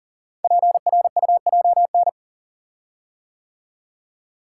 Morse code, heard as a keyed tone, about 42 wpm, 700 Hz. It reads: PRUJN
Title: Code Groups - 5 Letters - 40wpm